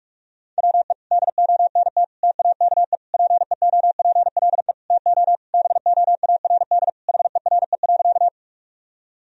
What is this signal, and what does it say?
Morse code, heard as a keyed tone, about 46 wpm, 700 Hz. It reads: WE DONT TAKE PEOPLE TO BOARD HERE.
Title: Sentences from Top 1000 Words - Review Part 4 - 45wpm